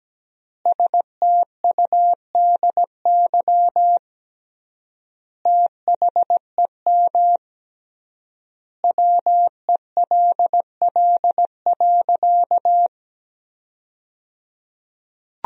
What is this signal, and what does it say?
Morse code, heard as a keyed tone, about 17 wpm, 700 Hz. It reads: STUDY THEM WELL.
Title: 3x Word-Spacing - Sentences from Top 300 Words - 17wpm